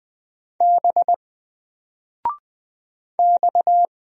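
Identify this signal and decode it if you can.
Morse code, heard as a keyed tone, about 20 wpm, 700 Hz. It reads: B X
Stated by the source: Single letter and number Morse drill at 20 wpm, from N4LCD.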